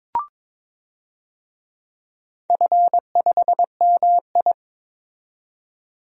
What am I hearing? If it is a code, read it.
Morse code, heard as a keyed tone, about 22 wpm, 700 Hz. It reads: F5MI
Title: DXCC-100 Call Signs - 4 Characters - 22wpm